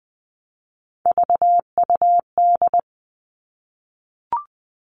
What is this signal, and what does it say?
Morse code, heard as a keyed tone, about 20 wpm, 700 Hz. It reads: VUD
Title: All Cut Numbers - 3-Digit 20wpm